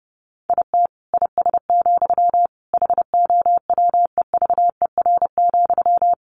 Morse code, heard as a keyed tone, about 30 wpm, 700 Hz. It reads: IT IS, HOWEVER,
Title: Sentences from Top 1400 Words 30wpm